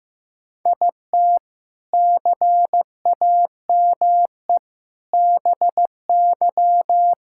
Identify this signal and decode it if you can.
Morse code, heard as a keyed tone, about 15 wpm, 700 Hz. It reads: IT CAME BY
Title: Sentences from Top 1000 Words - Review Part 5 - 15wpm